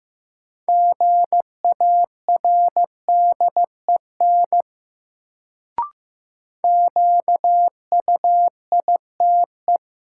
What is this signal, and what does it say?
Morse code, heard as a keyed tone, about 15 wpm, 700 Hz. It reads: GARDEN QUITE
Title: Sentences from Top 700 Words - New Word Drill - 15wpm